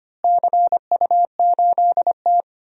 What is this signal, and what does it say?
Morse code, heard as a keyed tone, about 25 wpm, 700 Hz. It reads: CU8T
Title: DXCC-100 Call Signs - 4 Characters - Repeated 3x - 25wpm